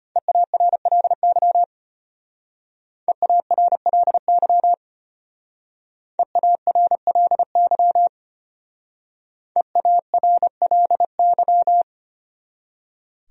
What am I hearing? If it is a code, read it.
Morse code, heard as a keyed tone, about 32 wpm, 700 Hz. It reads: EARLY EARLY EARLY EARLY